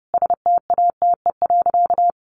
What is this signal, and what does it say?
Morse code, heard as a keyed tone, about 30 wpm, 700 Hz. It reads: STATE.
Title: Sentences from Top 1400 Words 30wpm